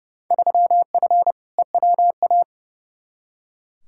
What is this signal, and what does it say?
Morse code, heard as a keyed tone, about 30 wpm, 700 Hz. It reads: 3F EWA
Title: Field Day Exchanges Encore - 30wpm